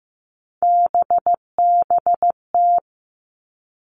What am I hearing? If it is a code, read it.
Morse code, heard as a keyed tone, about 15 wpm, 700 Hz. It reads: BBT
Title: Three Challenging Letter-Number Combinations 15wpm